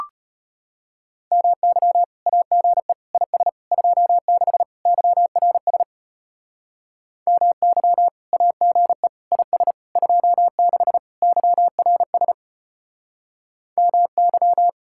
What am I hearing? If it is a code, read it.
Morse code, heard as a keyed tone, about 30 wpm, 700 Hz. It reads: MY AGE IS 26 YRS MY AGE IS 26 YRS MY
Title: Speed-Racing Common QSO Phrases - 25wpm